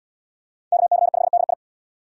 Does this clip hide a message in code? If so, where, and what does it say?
Yes: Morse code from 0.7 to 1.5 s, heard as a keyed tone, about 70 wpm, 700 Hz. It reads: KC5DI